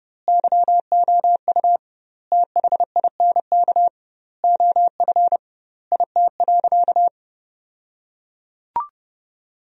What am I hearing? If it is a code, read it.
Morse code, heard as a keyed tone, about 32 wpm, 700 Hz. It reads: YOU THINK OF IT.
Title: 2x Word-Spacing - Sentences from Top 200 Words - 30wpm